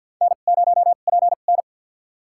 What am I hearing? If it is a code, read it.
Morse code, heard as a keyed tone, about 48 wpm, spaced as for 35 wpm, 700 Hz. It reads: N0PN
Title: USA Call Signs - 2x Character Spacing - 50wpm